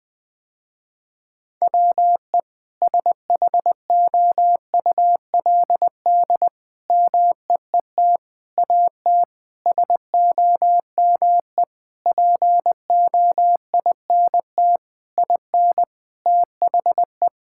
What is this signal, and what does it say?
Morse code, heard as a keyed tone, about 20 wpm, 700 Hz. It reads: WE SHOULD MEET AT SOME POINT IN THE
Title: Sentences with 7-Word Phrases - Spoken Beginning - 20wpm